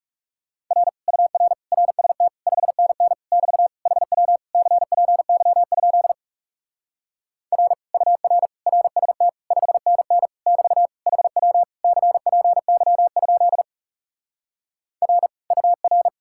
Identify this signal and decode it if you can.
Morse code, heard as a keyed tone, about 40 wpm, 700 Hz. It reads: R UR RST 5NN <BT> HW CPY? R UR RST 5NN <BT> HW CPY? R UR